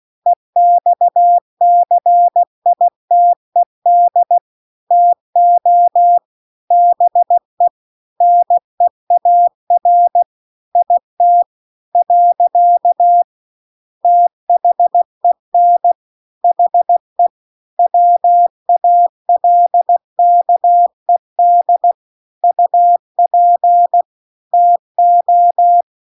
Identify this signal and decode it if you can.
Morse code, heard as a keyed tone, about 16 wpm, 700 Hz. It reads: EXCITED TO BE NEAR IT . THEN HE WALKED UP TO
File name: CW-The-Big-Twoheaded-River-2-16WPM